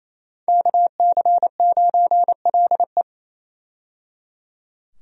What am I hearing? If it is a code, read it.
Morse code, heard as a keyed tone, about 27 wpm, 700 Hz. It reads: KC9LE
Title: USA Call Signs - 5 Characters - 28wpm